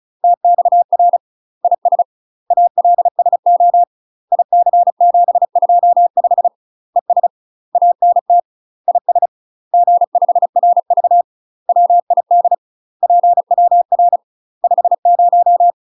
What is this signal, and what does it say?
Morse code, heard as a keyed tone, about 35 wpm, 700 Hz. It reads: TXR IS ALSO IC725 ES ANT IS G5RV WID PWR 50
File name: UK MASTER QSOs TRAINING / 35WPM / UK MASTER QSOs 0041